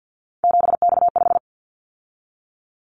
Morse code, heard as a keyed tone, about 50 wpm, 700 Hz. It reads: 7X5
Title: Callsign Prefixes for Top 100 DXCC - 50wpm